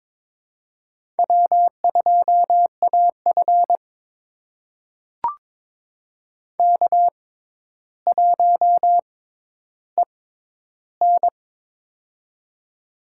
Morse code, heard as a keyed tone, about 21 wpm, spaced as for 11 wpm, 700 Hz. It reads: W2AF K1EN
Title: USA Call Signs - 4 Characters - 6x Character Spacing - 22wpm